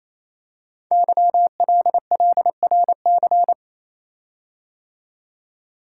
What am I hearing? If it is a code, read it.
Morse code, heard as a keyed tone, about 27 wpm, 700 Hz. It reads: YLLRC
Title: Code Groups - 5 Letters - 28wpm